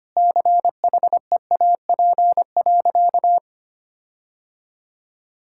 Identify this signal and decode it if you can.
Morse code, heard as a keyed tone, about 25 wpm, 700 Hz. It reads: CHEAP.
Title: Sentences from Top 1600 Words 25wpm